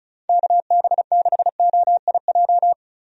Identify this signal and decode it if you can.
Morse code, heard as a keyed tone, about 34 wpm, 700 Hz. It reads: KB6OIJ
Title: USA Call Signs - 6 Characters - 35wpm